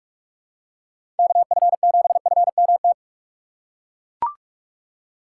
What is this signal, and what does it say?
Morse code, heard as a keyed tone, about 41 wpm, 700 Hz. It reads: KF7FMT
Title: USA Call Signs - 6 Characters - 4x Character Spacing - 45wpm